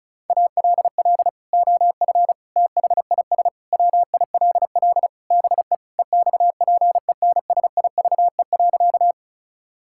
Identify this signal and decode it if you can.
Morse code, heard as a keyed tone, about 36 wpm, 700 Hz. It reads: ALL OF THIS WILL BE EXPENSIVE.
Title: Sentences from Top 1000 Words - Review Part 6 - 35wpm